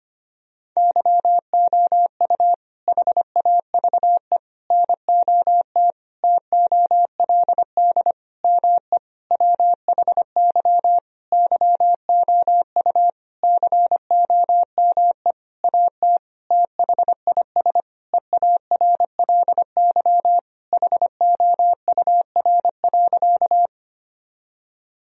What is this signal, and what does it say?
Morse code, heard as a keyed tone, about 25 wpm, 700 Hz. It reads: YOU HAVE NOT TOLD ME WHY YOU COME AT THIS EARLY HOUR.
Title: Sentences from Top 1000 Words - Review Part 2 - 25wpm